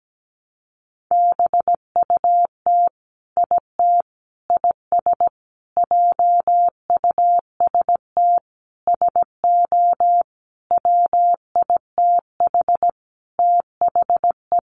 Morse code, heard as a keyed tone, about 17 wpm, 700 Hz. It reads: BUT IT IS JUST SO WITH THE
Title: Sentences from Top 200 Words 17wpm